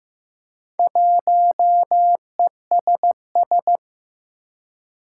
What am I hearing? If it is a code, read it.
Morse code, heard as a keyed tone, about 15 wpm, 700 Hz. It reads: 1ESS
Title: Four Challenging Letter-Number Combinations 15wpm